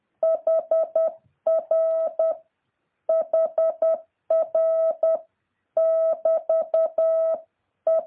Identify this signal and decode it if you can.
Morse code, heard as a keyed tone, about 10 wpm, 650 Hz. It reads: HR HR <BT> E